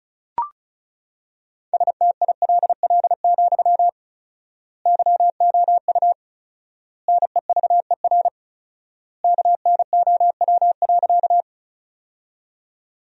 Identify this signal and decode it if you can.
Morse code, heard as a keyed tone, about 34 wpm, 700 Hz. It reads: STILL, YOU NEVER KNOW.
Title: 4x Word-Spacing - Sentences from Top 300 Words - 35wpm